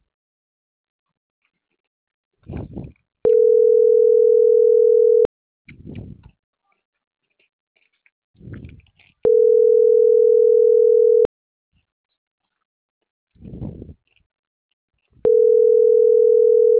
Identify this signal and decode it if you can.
Ringback tone (North America), 440 + 480 Hz.